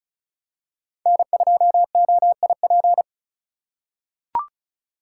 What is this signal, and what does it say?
Morse code, heard as a keyed tone, about 36 wpm, 700 Hz. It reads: N2OIP
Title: USA Call Signs - 5 Characters - 4x Character Spacing - 35wpm